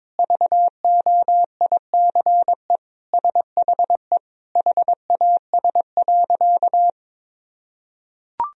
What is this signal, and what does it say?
Morse code, heard as a keyed tone, about 22 wpm, 700 Hz. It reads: VOICE SHE HAS.
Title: Sentences from Top 500 Words and Abbreviations 22wpm